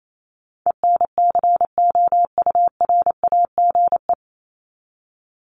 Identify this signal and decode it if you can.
Morse code, heard as a keyed tone, about 28 wpm, 700 Hz. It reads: ENCOURAGE
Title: Top 1251-1500 Words 28wpm